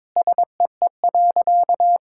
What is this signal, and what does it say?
Morse code, heard as a keyed tone, about 22 wpm, 700 Hz. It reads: SEE.
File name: Sentences with 7-Word Phrases - 22wpm